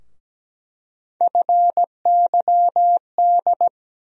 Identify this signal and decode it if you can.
Morse code, heard as a keyed tone, about 17 wpm, 700 Hz. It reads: FYD
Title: USA Call Sign Postfixes - 3 Letters 17wpm